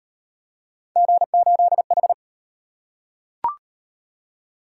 Morse code, heard as a keyed tone, about 30 wpm, 700 Hz. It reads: G8H
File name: Speed-Racing - Three Letter-Number Combinations - 25wpm